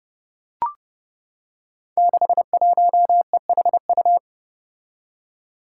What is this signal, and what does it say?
Morse code, heard as a keyed tone, about 31 wpm, 700 Hz. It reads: 61EHU
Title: Five Challenging Letter-Number Combinations 30wpm